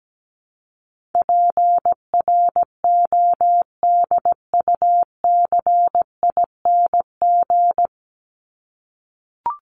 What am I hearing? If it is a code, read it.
Morse code, heard as a keyed tone, about 17 wpm, 700 Hz. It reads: PRODUCING
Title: ING Suffix - Top 100 Words with Base Words 17wpm